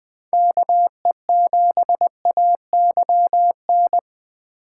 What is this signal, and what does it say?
Morse code, heard as a keyed tone, about 20 wpm, 700 Hz. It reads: KE7AYN
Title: Speed Racing - USA 2x3 Call Signs 17wpm